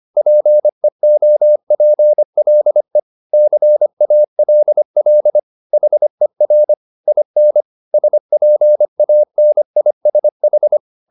Morse code, heard as a keyed tone, about 25 wpm, 600 Hz. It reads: PEOPLE CALL HER IN SPANISH
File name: the old man and the sea part 1 25wpm